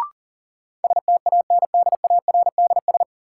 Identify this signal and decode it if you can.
Morse code, heard as a keyed tone, about 40 wpm, 700 Hz. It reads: STANDARDS
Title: Sentences from Top 1000 Words - Review Part 6 - New Word Drill - 40wpm